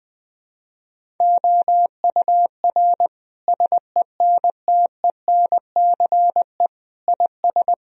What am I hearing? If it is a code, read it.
Morse code, heard as a keyed tone, about 21 wpm, 700 Hz. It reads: OUR SENTENCE IS
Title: Sentences from Top 1000 Words - Review Part 1 - 20wpm